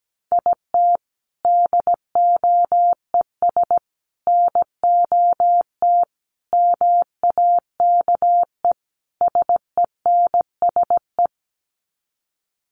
Morse code, heard as a keyed tone, about 17 wpm, 700 Hz. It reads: IT DOES NOT MAKE SENSE
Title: Sets of 5 Words Encore2 17wpm